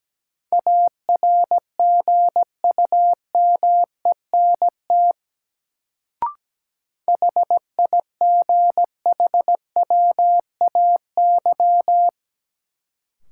Morse code, heard as a keyed tone, about 17 wpm, 700 Hz. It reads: ARGUMENT HIGHWAY
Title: Sentences from Top 1400 Words - New Word Drill Alt - 17wpm